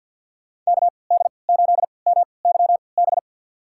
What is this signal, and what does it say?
Morse code, heard as a keyed tone, about 50 wpm, spaced as for 28 wpm, 700 Hz. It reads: KD8MYB
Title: USA Call Signs - 6 Characters - 3x Character Spacing - 50wpm